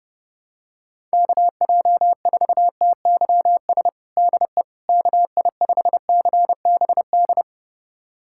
Morse code, heard as a keyed tone, about 30 wpm, 700 Hz. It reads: KJ4TYS DE KI5CBD
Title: USA Callsign DE Callsign - 6 Characters - 30wpm